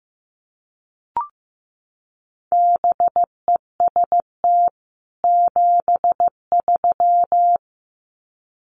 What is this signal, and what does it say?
Morse code, heard as a keyed tone, about 15 wpm, 700 Hz. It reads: BEST 73
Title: Common QSO Phrases 15wpm